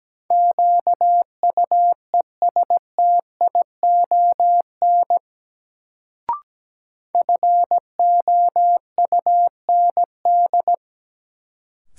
Morse code, heard as a keyed tone, about 17 wpm, 700 Hz. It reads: QUESTION FOUND
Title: Sets of 5 Words Encore - New Word Drill - 17wpm